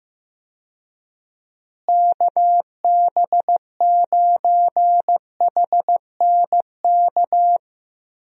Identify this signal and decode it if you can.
Morse code, heard as a keyed tone, about 15 wpm, 700 Hz. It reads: KB9HNK
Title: USA Call Signs - 6 Characters - Repeated 3x - 15wpm